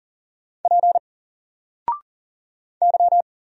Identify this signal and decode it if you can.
Morse code, heard as a keyed tone, about 40 wpm, 700 Hz. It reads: P Y
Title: Single Letter-Number 40wpm